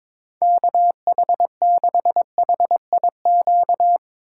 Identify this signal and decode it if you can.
Morse code, heard as a keyed tone, about 23 wpm, 700 Hz. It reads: KH6HIQ
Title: USA Call Signs - 6 Characters - 4x Character Spacing - 22wpm